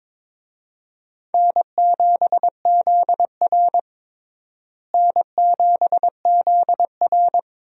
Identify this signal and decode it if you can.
Morse code, heard as a keyed tone, about 22 wpm, 700 Hz. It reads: N7ZR N7ZR
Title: USA Call Signs - 4 Characters - Repeated 2x - 22wpm